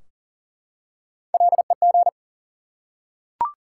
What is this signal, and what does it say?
Morse code, heard as a keyed tone, about 39 wpm, 700 Hz. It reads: LEG
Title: Three Letter Words - 40wpm